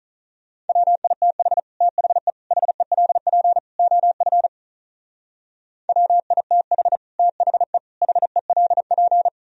Morse code, heard as a keyed tone, about 43 wpm, 700 Hz. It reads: WITH THE HELP OF WITH THE HELP
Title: Speed-Racing - Sets of 4 Words 35wpm